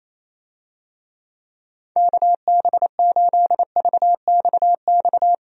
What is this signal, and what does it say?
Morse code, heard as a keyed tone, about 28 wpm, 700 Hz. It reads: KB8VXX
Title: USA Call Signs - 6 Characters - Repeated 2x - 28wpm